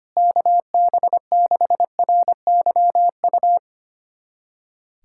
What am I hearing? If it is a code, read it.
Morse code, heard as a keyed tone, about 25 wpm, 700 Hz. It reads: KB6RYU
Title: USA Callsign DE Callsign - 6 Characters - 25wpm